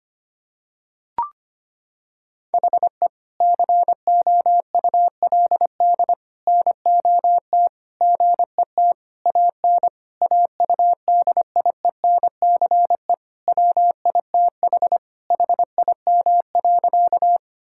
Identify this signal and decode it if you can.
Morse code, heard as a keyed tone, about 25 wpm, 700 Hz. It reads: HE COULD NOT GET AN AUDIENCE WITH HIM.
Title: Sentences from Top 1000 Words - Review Part 5 - 25wpm